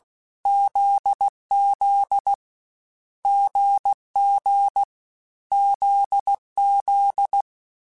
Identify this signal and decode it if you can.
Morse code, heard as a keyed tone, about 16 wpm, 800 Hz. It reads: ZZ GG ZZ
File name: Difficult Combos / Difficult Combos-mp3s / GZ GZ7 @15